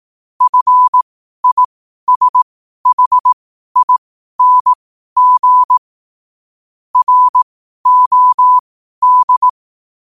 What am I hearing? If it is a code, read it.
Morse code, heard as a keyed tone, about 18 wpm, spaced as for 12 wpm, 1000 Hz. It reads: FISHING ROD
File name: qotdfiles / 12wpm